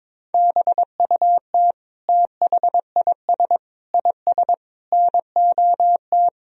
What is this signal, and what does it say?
Morse code, heard as a keyed tone, about 22 wpm, 700 Hz. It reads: BUT THIS IS NOT
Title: Sentences from Top 1000 Words - Review Part 2 - 22wpm